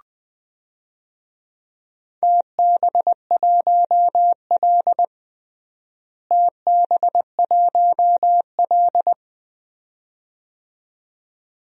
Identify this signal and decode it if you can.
Morse code, heard as a keyed tone, about 20 wpm, 700 Hz. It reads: TB1L TB1L
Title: DXCC-100 Call Signs - 4 Characters - Repeated 2x - 20wpm